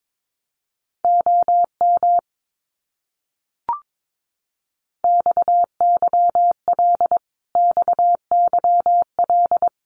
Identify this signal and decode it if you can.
Morse code, heard as a keyed tone, about 22 wpm, 700 Hz. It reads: OM XYL XYL
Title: Abbreviations Repeated 3x - 22wpm